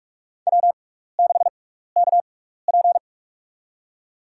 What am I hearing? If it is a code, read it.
Morse code, heard as a keyed tone, about 44 wpm, spaced as for 15 wpm, 700 Hz. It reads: W6KP
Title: USA Call Signs - 6x Character Spacing - 45wpm